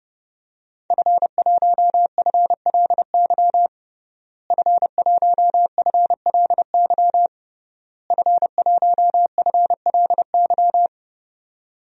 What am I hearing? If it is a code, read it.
Morse code, heard as a keyed tone, about 32 wpm, 700 Hz. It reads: F1FLY F1FLY F1FLY
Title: DXCC-100 Call Signs - 5 Characters - Repeated 3x - 30wpm